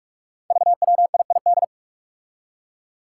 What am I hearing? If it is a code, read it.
Morse code, heard as a keyed tone, about 46 wpm, 700 Hz. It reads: VWIID
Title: Five Challenging Letter-Number Combinations 45wpm